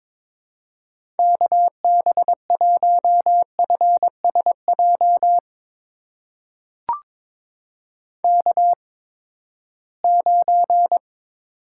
Morse code, heard as a keyed tone, about 21 wpm, spaced as for 15 wpm, 700 Hz. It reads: KB1FSJ K9
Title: USA Call Signs - 8x Character Spacing - 22wpm